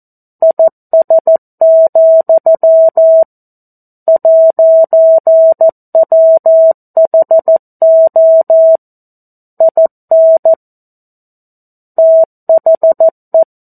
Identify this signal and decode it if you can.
Morse code, heard as a keyed tone, about 14 wpm, 650 Hz. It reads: IS, 'WHO IN THE
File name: 02-Chapter-02 14 WPM